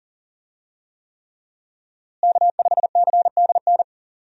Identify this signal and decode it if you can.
Morse code, heard as a keyed tone, about 40 wpm, 700 Hz. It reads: K5CDN
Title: USA Call Signs - 5 Characters - Repeated 3x - 40wpm